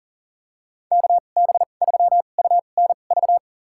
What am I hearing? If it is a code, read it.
Morse code, heard as a keyed tone, about 40 wpm, spaced as for 29 wpm, 700 Hz. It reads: KB3UNV
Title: USA Call Signs - 2x Character Spacing - 40wpm